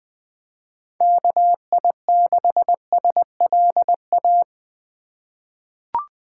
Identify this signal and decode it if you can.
Morse code, heard as a keyed tone, about 20 wpm, 700 Hz. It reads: KI6SLA
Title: 2023 SOTA Chaser Call Signs - 20wpm